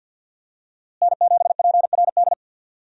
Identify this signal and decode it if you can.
Morse code, heard as a keyed tone, about 51 wpm, 700 Hz. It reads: N7PRD
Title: USA Call Signs - 5 Characters - Repeated 2x - 50wpm